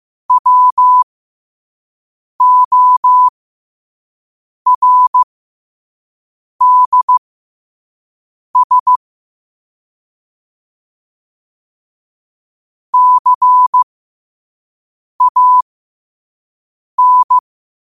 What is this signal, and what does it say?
Morse code, heard as a keyed tone, about 15 wpm, spaced as for 5 wpm, 1000 Hz. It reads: WORDS CAN
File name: qotdfiles / 5wpm